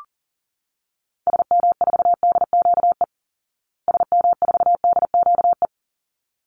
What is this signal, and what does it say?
Morse code, heard as a keyed tone, about 40 wpm, 700 Hz. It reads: SM4DQE SM4DQE
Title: Call Signs Repeated 3x - 40wpm